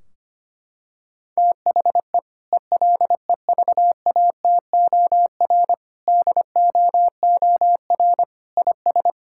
Morse code, heard as a keyed tone, about 25 wpm, 700 Hz. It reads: THE ELEVATOR DOOR IS
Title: Sentences from Top 1600 Words 25wpm